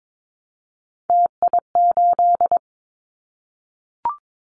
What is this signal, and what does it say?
Morse code, heard as a keyed tone, about 22 wpm, 700 Hz. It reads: TI8
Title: Callsign Prefixes for Top 100 DXCC - 22wpm